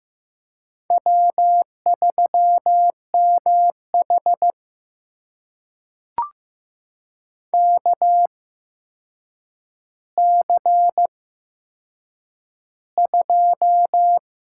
Morse code, heard as a keyed tone, about 15 wpm, spaced as for 7 wpm, 700 Hz. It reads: W3MH KC2
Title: USA Call Signs - 8x Character Spacing - 15wpm